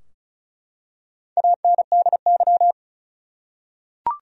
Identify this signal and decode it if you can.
Morse code, heard as a keyed tone, about 36 wpm, 700 Hz. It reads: ANDY
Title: USA Common Names - 35wpm